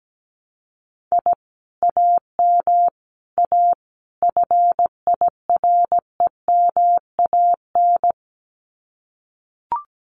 Morse code, heard as a keyed tone, about 17 wpm, 700 Hz. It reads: I AM A FIREMAN
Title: Common QSO Phrases Encore 17wpm